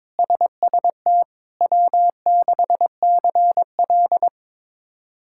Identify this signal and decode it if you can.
Morse code, heard as a keyed tone, about 21 wpm, 700 Hz. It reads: SST W6CL
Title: K1USN SST QSO Practice - 22wpm